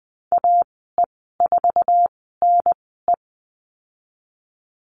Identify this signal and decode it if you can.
Morse code, heard as a keyed tone, about 20 wpm, spaced as for 15 wpm, 700 Hz. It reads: AE4NE
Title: 055 Lesson 10g - Letter r - Call Signs 20wpm